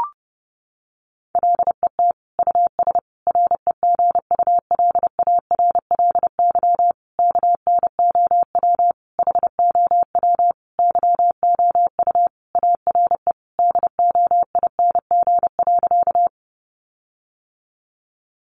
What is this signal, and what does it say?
Morse code, heard as a keyed tone, about 30 wpm, 700 Hz. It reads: LET US REGULARLY KNOW HOW YOU ARE DOING.
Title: Sentences from Top 1400 Words 30wpm